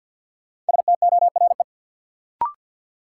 Morse code, heard as a keyed tone, about 52 wpm, 700 Hz. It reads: STORE
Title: Sentences from Top 1000 Words - Review Part 5 - New Word Drill - 50wpm